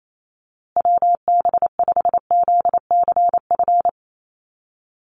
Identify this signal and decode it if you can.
Morse code, heard as a keyed tone, about 28 wpm, 700 Hz. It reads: WB5ZCF
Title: USA Call Signs Repeated 3x 28wpm